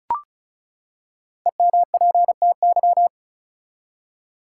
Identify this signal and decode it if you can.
Morse code, heard as a keyed tone, about 34 wpm, 700 Hz. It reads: EMPTY
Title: Five Letter Words - 35wpm